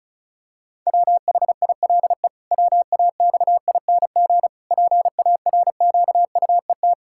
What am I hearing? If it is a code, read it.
Morse code, heard as a keyed tone, about 36 wpm, 700 Hz. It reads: WHILE WAXING PARQUET
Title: Pangram Challenge v2 - 35wpm